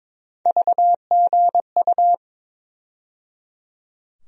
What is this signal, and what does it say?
Morse code, heard as a keyed tone, about 22 wpm, 700 Hz. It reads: VGU
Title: USA Call Sign Postfixes - 3 Letters 22wpm